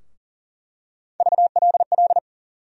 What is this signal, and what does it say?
Morse code, heard as a keyed tone, about 40 wpm, 700 Hz. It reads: VLL
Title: USA Call Sign Postfixes - 3 Letters 40wpm